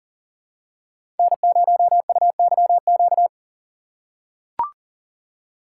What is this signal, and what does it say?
Morse code, heard as a keyed tone, about 40 wpm, 700 Hz. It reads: N0UYQ
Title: USA Call Signs - 5 Characters - 2x Character Spacing - 40wpm